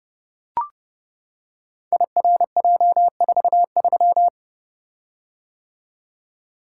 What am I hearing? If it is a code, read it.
Morse code, heard as a keyed tone, about 29 wpm, 700 Hz. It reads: IRJ43
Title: Code Groups - 5 Letter:Numbers - 30wpm